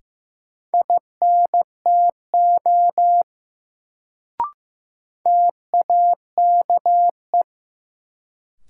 Morse code, heard as a keyed tone, about 15 wpm, 700 Hz. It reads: INTO TAKE
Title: Sets of 2 Words - New Word Drill - 15wpm